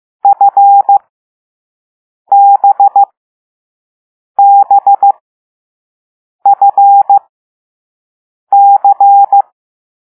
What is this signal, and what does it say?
Morse code, heard as a keyed tone, about 15 wpm, spaced as for 5 wpm, 800 Hz. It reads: FBBFC
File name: Morse / Level6B